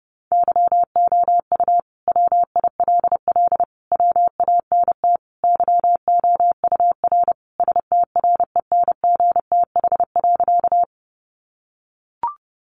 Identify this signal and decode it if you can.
Morse code, heard as a keyed tone, about 30 wpm, 700 Hz. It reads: YOU WILL WANT YOUR STRENGTH.
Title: Sentences from Top 1300 Words 30wpm